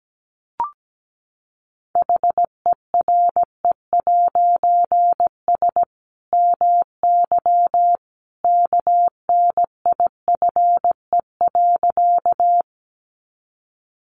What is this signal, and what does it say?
Morse code, heard as a keyed tone, about 17 wpm, 700 Hz. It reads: HERE'S MY KNIFE.
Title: A Sound Of Thunder 17wpm